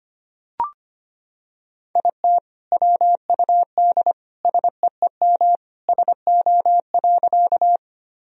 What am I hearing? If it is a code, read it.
Morse code, heard as a keyed tone, about 25 wpm, 700 Hz. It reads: IT WUD SEEM SO.
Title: Sentences from Top 300 Words and Abbreviations 25wpm